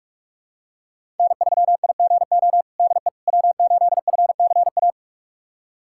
Morse code, heard as a keyed tone, about 43 wpm, 700 Hz. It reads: N3IGO DE W8FCA
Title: USA Callsign DE Callsign - 5 Characters - 45wpm